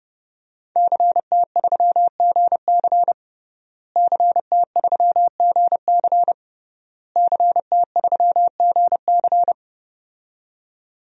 Morse code, heard as a keyed tone, about 30 wpm, 700 Hz. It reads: CT3GC CT3GC CT3GC
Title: DXCC-100 Call Signs - 5 Characters - Repeated 3x - 30wpm